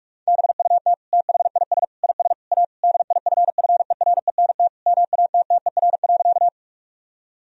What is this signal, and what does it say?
Morse code, heard as a keyed tone, about 46 wpm, 700 Hz. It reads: BUT THIS IS A DIFFERENT MATTER.
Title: Sentences from Top 1000 Words - Review Part 3 - 45wpm